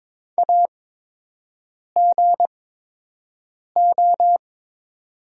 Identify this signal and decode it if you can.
Morse code, heard as a keyed tone, about 22 wpm, spaced as for 6 wpm, 700 Hz. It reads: AGO
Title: 8x Character Spacing - Top 500 Words - 22wpm